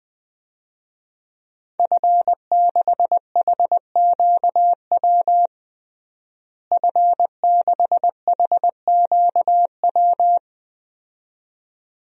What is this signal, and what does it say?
Morse code, heard as a keyed tone, about 20 wpm, 700 Hz. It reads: F6HQW F6HQW
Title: DXCC-100 Call Signs - 5 Characters - Repeated 2x - 20wpm